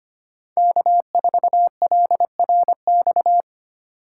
Morse code, heard as a keyed tone, about 25 wpm, 700 Hz. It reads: K4LRX
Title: K1USN SST QSO Practice - 25wpm